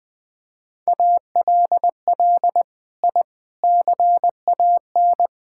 Morse code, heard as a keyed tone, about 20 wpm, 700 Hz. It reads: ALL I CAN
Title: Sentences with 6-Word Phrases - Spoken Beginning - 20wpm